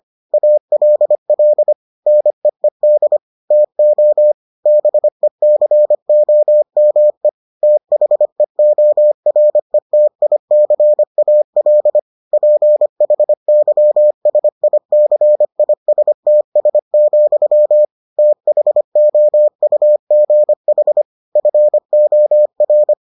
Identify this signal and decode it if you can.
Morse code, heard as a keyed tone, about 25 wpm, 600 Hz. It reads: ALL NEED TO BECOME THEORETICAL PHYSICISTS, THOUGH FOR